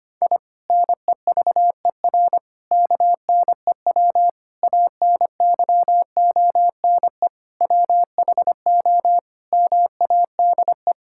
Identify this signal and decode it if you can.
Morse code, heard as a keyed tone, about 25 wpm, 700 Hz. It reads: I NEVER KNEW ANYONE WHO MADE
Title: Sentences from Top 1000 Words - Review Part 6 - 25wpm